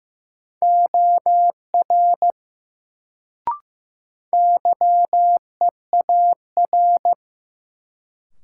Morse code, heard as a keyed tone, about 15 wpm, 700 Hz. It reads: OR YEAR
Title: Sets of 2 Words - New Word Drill - 15wpm